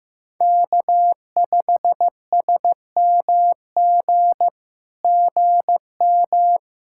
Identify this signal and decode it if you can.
Morse code, heard as a keyed tone, about 15 wpm, 700 Hz. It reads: K5SMG GM
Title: SOTA QSO Practice 15wpm